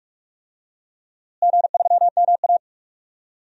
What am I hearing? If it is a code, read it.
Morse code, heard as a keyed tone, about 41 wpm, 700 Hz. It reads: G3MA
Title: DXCC-100 Call Signs - 4 Characters - Repeated 3x - 45wpm